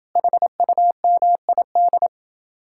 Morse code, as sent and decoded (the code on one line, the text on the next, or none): .... ..- -- .. -..
HUMID